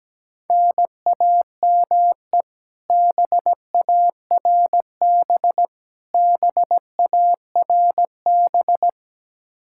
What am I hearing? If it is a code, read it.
Morse code, heard as a keyed tone, about 17 wpm, 700 Hz. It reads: NAME BARB BARB
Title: Practice QSO - Exchange 1 - Beginning - 17wpm